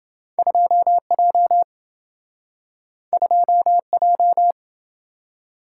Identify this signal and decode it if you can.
Morse code, heard as a keyed tone, about 26 wpm, 700 Hz. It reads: 2J 2J